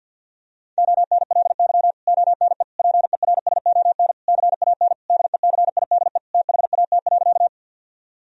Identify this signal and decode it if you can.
Morse code, heard as a keyed tone, about 50 wpm, 700 Hz. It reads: ONLY ONE PERSON CAN DECIDE THAT.
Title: Sentences from Top 1000 Words - Review Part 3 - 50wpm